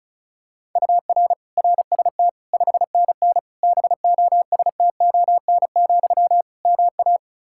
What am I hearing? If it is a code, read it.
Morse code, heard as a keyed tone, about 36 wpm, 700 Hz. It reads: UR RST 5NN BOSTON, MA
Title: Practice QSO - Exchange 1 - Beginning - 35wpm